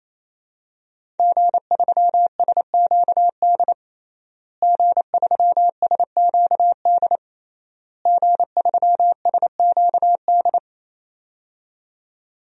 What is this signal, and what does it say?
Morse code, heard as a keyed tone, about 28 wpm, 700 Hz. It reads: G3SQD G3SQD G3SQD
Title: DXCC-100 Call Signs - 5 Characters - Repeated 3x - 28wpm